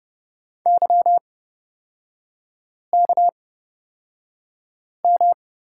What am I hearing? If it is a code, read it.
Morse code, heard as a keyed tone, about 30 wpm, spaced as for 5 wpm, 700 Hz. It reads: YKM
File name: Lesson 38e - Letter q - Three Characters 30wpm